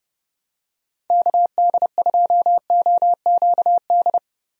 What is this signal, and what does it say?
Morse code, heard as a keyed tone, about 30 wpm, 700 Hz. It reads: KD2OQD
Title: USA Call Signs - 6 Characters - Repeated 3x - 30wpm